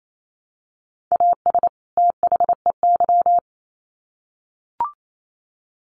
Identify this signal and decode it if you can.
Morse code, heard as a keyed tone, about 28 wpm, 700 Hz. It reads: AS THEY
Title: Sets of 2 Words 28wpm